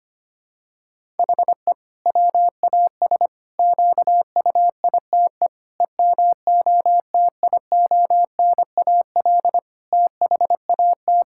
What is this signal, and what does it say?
Morse code, heard as a keyed tone, about 25 wpm, 700 Hz. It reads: HE WAS QUITE EMOTIONAL THAT
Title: Sentences from Top 1000 Words - Review Part 5 - 25wpm